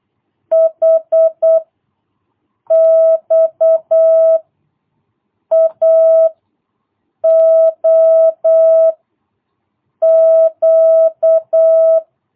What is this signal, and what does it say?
Morse code, heard as a keyed tone, about 8 wpm, 650 Hz. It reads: H X A O Q
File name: jh8cbh / 2022cw